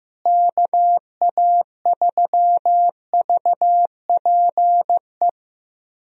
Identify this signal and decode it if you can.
Morse code, heard as a keyed tone, about 15 wpm, 700 Hz. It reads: KA3VPE
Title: USA Call Signs - 6 Characters - Repeated 3x - 15wpm